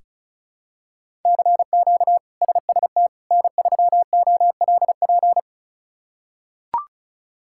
Morse code, heard as a keyed tone, about 36 wpm, 700 Hz. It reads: CQ SST N3OLP
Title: K1USN SST QSO Practice - 35wpm